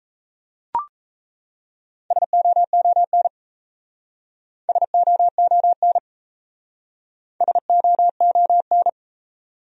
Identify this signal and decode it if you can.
Morse code, heard as a keyed tone, about 35 wpm, 700 Hz. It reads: SOON SOON SOON